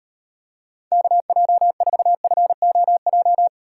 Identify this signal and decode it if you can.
Morse code, heard as a keyed tone, about 38 wpm, 700 Hz. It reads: KJ4FOJ